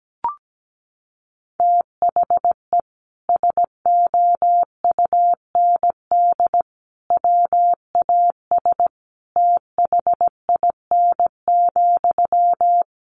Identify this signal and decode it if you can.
Morse code, heard as a keyed tone, about 17 wpm, 700 Hz. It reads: THE SOUND WAS THIN,
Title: Sentences from Top 1200 Words 17wpm